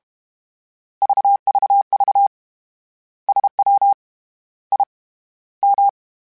Morse code, heard as a keyed tone, about 32 wpm, 800 Hz. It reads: VVV SW I M